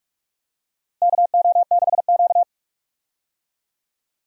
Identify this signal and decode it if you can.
Morse code, heard as a keyed tone, about 42 wpm, 700 Hz. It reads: KO6Q
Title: USA Call Signs - 4 Characters - Repeated 2x - 45wpm